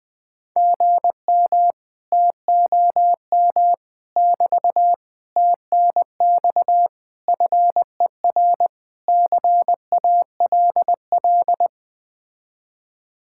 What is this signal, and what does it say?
Morse code, heard as a keyed tone, about 20 wpm, 700 Hz. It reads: GM TOM <BT> TNX FER CALL
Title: Speed-Racing Common QSO Phrases Encore - 17wpm